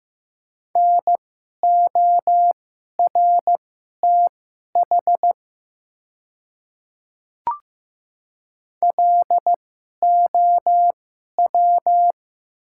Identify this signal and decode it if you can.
Morse code, heard as a keyed tone, about 15 wpm, spaced as for 11 wpm, 700 Hz. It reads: NORTH LOW
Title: 2x Character Spacing - Top 300 Words - 15wpm